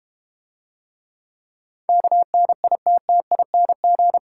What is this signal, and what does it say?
Morse code, heard as a keyed tone, about 30 wpm, 690 Hz. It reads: KNITTING
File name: Easy Words - Speed-Racing with Hints - 25wpm